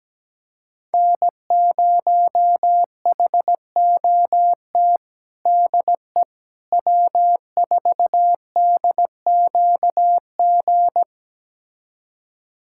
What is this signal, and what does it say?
Morse code, heard as a keyed tone, about 17 wpm, 700 Hz. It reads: N0HOT DE W4DQG
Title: USA Callsign DE Callsign - 5 Characters - 17wpm